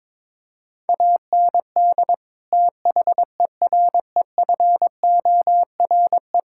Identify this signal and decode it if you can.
Morse code, heard as a keyed tone, about 22 wpm, 700 Hz. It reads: AND THEREFORE
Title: Sentences from Top 1000 Words - Review Part 5 - 22wpm